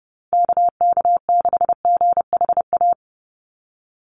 Morse code, heard as a keyed tone, about 30 wpm, 700 Hz. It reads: KK6GHA
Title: Call Signs - 30wpm